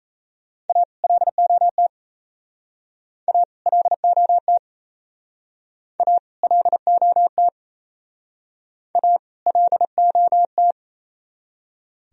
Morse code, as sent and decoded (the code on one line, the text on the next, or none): .- / .-.. --- - / .- / .-.. --- - / .- / .-.. --- - / .- / .-.. --- -
A LOT A LOT A LOT A LOT